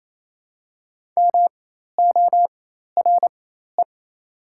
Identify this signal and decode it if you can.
Morse code, heard as a keyed tone, about 28 wpm, spaced as for 13 wpm, 700 Hz. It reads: MORE
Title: 4x Character Spacing - Top 100 Words - 28wpm